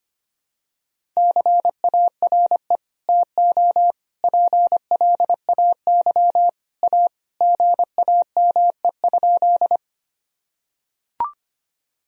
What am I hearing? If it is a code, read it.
Morse code, heard as a keyed tone, about 25 wpm, 700 Hz. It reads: CARE TO PLAY A GAME?
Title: Sentences from Top 1000 Words - Review Part 1 - 25wpm